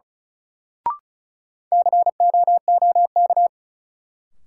Morse code, heard as a keyed tone, about 36 wpm, 700 Hz. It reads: COOK
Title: Sentences from Top 700 Words - New Word Drill - 35wpm